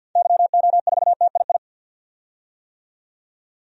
Morse code, heard as a keyed tone, about 52 wpm, 700 Hz. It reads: YO4TII
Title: DXCC-100 Call Signs - 6 Characters - 50wpm